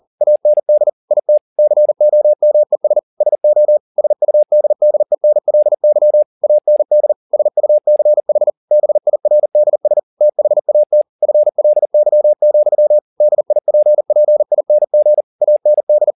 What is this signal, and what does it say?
Morse code, heard as a keyed tone, about 40 wpm, 600 Hz. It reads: AND IT COMES SO SUDDENLY AND SUCH BIRDS THAT FLY, DIPPING AND